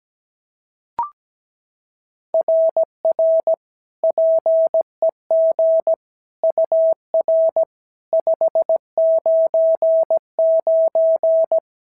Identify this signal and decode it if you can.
Morse code, heard as a keyed tone, about 16 wpm, 650 Hz. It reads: RR PEG UR 599
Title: SOTA QSO Practice 17wpm